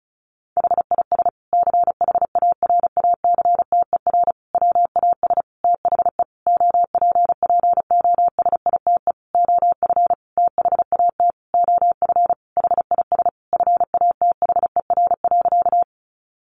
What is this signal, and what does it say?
Morse code, heard as a keyed tone, about 34 wpm, 700 Hz. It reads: HIS CHARACTER WAS THE OPPOSITE OF THAT OF HIS FATHER.
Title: Sentences from Top 1300 Words 35wpm